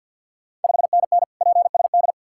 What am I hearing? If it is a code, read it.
Morse code, heard as a keyed tone, about 51 wpm, 690 Hz. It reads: 5NN PSD
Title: Ontario QSO Party Exchange 50wpm